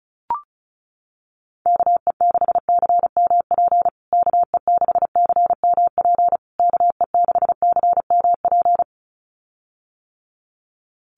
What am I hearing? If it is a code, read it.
Morse code, heard as a keyed tone, about 34 wpm, 700 Hz. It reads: KE6CMP KE6CMP KE6CMP
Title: USA Call Signs Repeated 3x 35wpm